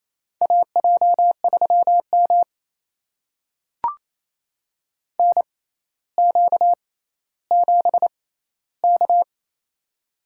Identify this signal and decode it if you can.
Morse code, heard as a keyed tone, about 28 wpm, spaced as for 14 wpm, 700 Hz. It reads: AJ3M NQ7K
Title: USA Call Signs - 4 Characters - 6x Character Spacing - 28wpm